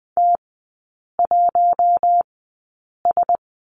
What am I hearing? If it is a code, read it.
Morse code, heard as a keyed tone, about 20 wpm, spaced as for 8 wpm, 700 Hz. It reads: T1S
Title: 074 Lesson 13e - Letter L - Three Characters 20wpm